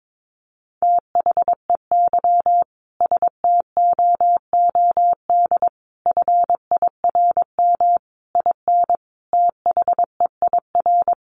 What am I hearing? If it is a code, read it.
Morse code, heard as a keyed tone, about 22 wpm, 700 Hz. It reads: THEY STOOD FIRM IN THEIR
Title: Sentences from Top 1400 Words 22wpm